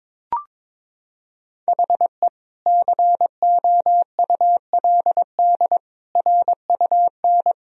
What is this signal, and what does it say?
Morse code, heard as a keyed tone, about 22 wpm, 700 Hz. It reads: HE COULD RUN
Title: Sentences from Top 1000 Words - Review Part 1 - 22wpm